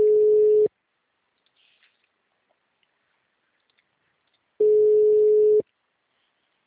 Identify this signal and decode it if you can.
Ringback tone (Europe, 425 Hz).